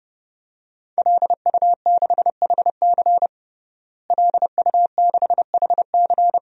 Morse code, heard as a keyed tone, about 29 wpm, 700 Hz. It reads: LU6HC LU6HC
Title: DXCC-100 Call Signs - 5 Characters - Repeated 2x - 30wpm